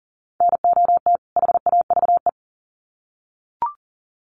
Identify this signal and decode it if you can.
Morse code, heard as a keyed tone, about 40 wpm, 700 Hz. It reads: NOT HAVE
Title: Sets of 2 Words 40wpm